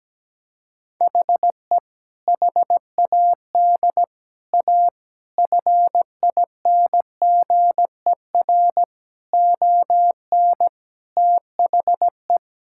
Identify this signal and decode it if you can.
Morse code, heard as a keyed tone, about 17 wpm, 700 Hz. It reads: HE HAD A FINGER ON THE
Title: Sentences from Top 1600 Words 17wpm